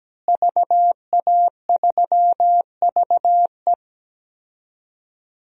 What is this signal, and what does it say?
Morse code, heard as a keyed tone, about 17 wpm, 700 Hz. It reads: VA3VE
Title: DXCC-100 Call Signs - 5 Characters - Repeated 3x - 17wpm